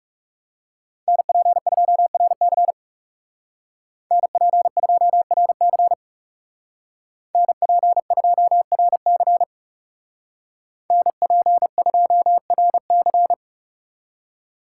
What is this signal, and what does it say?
Morse code, heard as a keyed tone, about 34 wpm, 700 Hz. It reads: NP2RC NP2RC NP2RC NP2RC
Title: Speed Racing - USA 2x2 Call Signs 30wpm